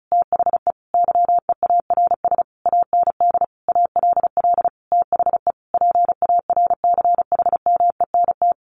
Morse code, heard as a keyed tone, about 34 wpm, 700 Hz. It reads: THE YEARS AND ALL THE PARCHMENT
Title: A Sound Of Thunder 35wpm